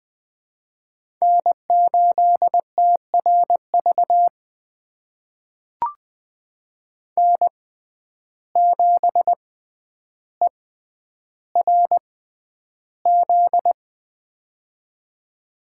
Morse code, heard as a keyed tone, about 20 wpm, spaced as for 10 wpm, 700 Hz. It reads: N8TRV N7ERZ
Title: USA Call Signs - 5 Characters - 6x Character Spacing - 20wpm